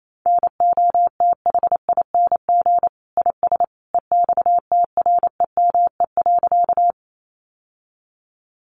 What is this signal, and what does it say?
Morse code, heard as a keyed tone, about 28 wpm, 700 Hz. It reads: NOTHING IS EXTREME.
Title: Sentences from Top 1300 Words 28wpm